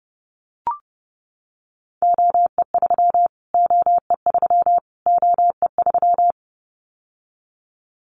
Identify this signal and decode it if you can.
Morse code, heard as a keyed tone, about 30 wpm, 700 Hz. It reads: OE3 OE3 OE3
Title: Callsign Prefixes for Top 25 DXCC - Repeated 3x - 30wpm